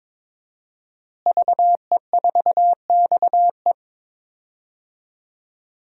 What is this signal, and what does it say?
Morse code, heard as a keyed tone, about 22 wpm, 700 Hz. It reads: VE4XE
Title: DXCC-100 Call Signs - 5 Characters - Repeated 2x - 22wpm